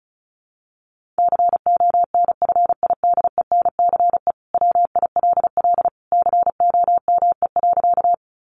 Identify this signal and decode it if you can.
Morse code, heard as a keyed tone, about 34 wpm, 700 Hz. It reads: CONFIDENCE WILL COME.
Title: Sentences from Top 1500 Words 35wpm